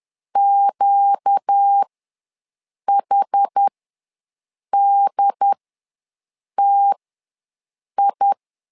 Morse code, heard as a keyed tone, about 10 wpm, spaced as for 6 wpm, 800 Hz. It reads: QHDTI
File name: LEZIONI / 4 Q1 ws 10wpm 2538